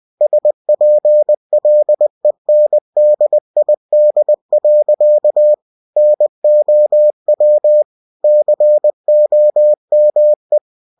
Morse code, heard as a keyed tone, about 20 wpm, 600 Hz. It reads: SPLENDID. NOW COME
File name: cw-books / winnie-the-pooh / winnie-20wpm-page-0066